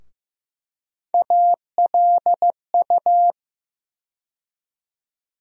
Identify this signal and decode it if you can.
Morse code, heard as a keyed tone, about 15 wpm, 700 Hz. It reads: ALU
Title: USA Call Sign Postfixes - 3 Letters 15wpm